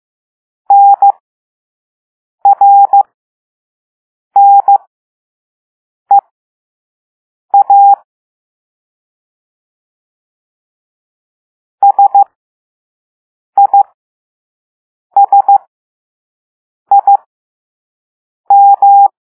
Morse code, heard as a keyed tone, about 15 wpm, spaced as for 5 wpm, 800 Hz. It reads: NRNEA SISIM
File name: Morse / Level3-All-B